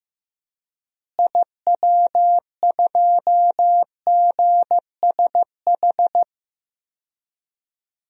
Morse code, heard as a keyed tone, about 15 wpm, 700 Hz. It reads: IW2GSH
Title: DXCC-100 Call Signs - 6 Characters - Repeated 3x - 15wpm